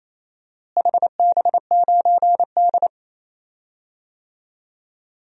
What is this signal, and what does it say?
Morse code, heard as a keyed tone, about 28 wpm, 700 Hz. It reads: HB9D
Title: DXCC-100 Call Signs - 4 Characters - Repeated 2x - 28wpm